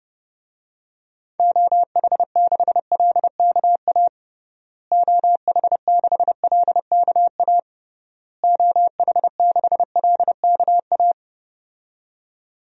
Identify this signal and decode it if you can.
Morse code, heard as a keyed tone, about 30 wpm, 700 Hz. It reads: OH6LKA OH6LKA OH6LKA
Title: DXCC-100 Call Signs - 6 Characters - Repeated 3x - 30wpm